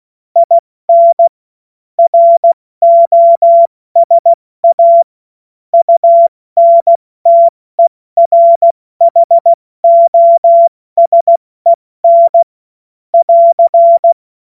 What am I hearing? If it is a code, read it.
Morse code, heard as a keyed tone, about 16 wpm, 680 Hz. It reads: IN ROSA UNTERHOSEN <AR>
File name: Morsekurs / Witze / Witz 10. August 2021, 70 F80